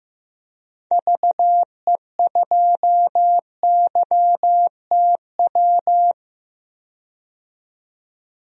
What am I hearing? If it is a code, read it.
Morse code, heard as a keyed tone, about 15 wpm, 700 Hz. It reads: VE2YTW
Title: DXCC-100 Call Signs - 6 Characters - 15wpm